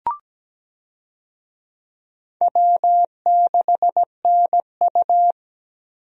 Morse code, heard as a keyed tone, about 17 wpm, 700 Hz. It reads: W6NU
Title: USA Call Signs - 4 Characters - Repeated 3x - 17wpm